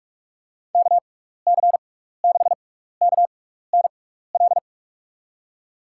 Morse code, heard as a keyed tone, about 42 wpm, spaced as for 15 wpm, 700 Hz. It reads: KC6KNL
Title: USA Call Signs - 6 Characters - 6x Character Spacing - 45wpm